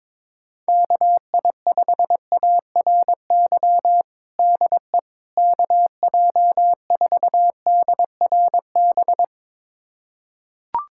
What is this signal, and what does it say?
Morse code, heard as a keyed tone, about 22 wpm, 700 Hz. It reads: KI5ARY DE KJ4DRB
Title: USA Callsign DE Callsign 22wpm